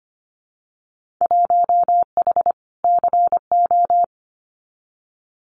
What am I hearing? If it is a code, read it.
Morse code, heard as a keyed tone, about 25 wpm, 700 Hz. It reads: 1H CO
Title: Winter Field Day Exchanges 25wpm